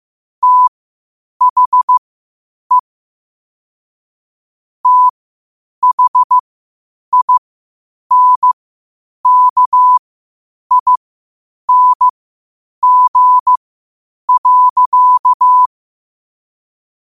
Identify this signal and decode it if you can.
Morse code, heard as a keyed tone, about 15 wpm, spaced as for 8 wpm, 1000 Hz. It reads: THE THINKING.
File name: qotdfiles / 8wpm